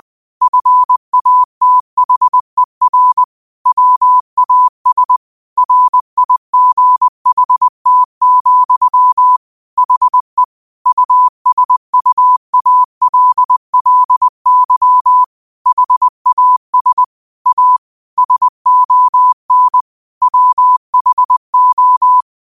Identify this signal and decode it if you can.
Morse code, heard as a keyed tone, about 20 wpm, 1000 Hz. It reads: FATHER WAS RIGHT, HE USUALLY HAS A SON WHO
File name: qotdfiles / 20wpm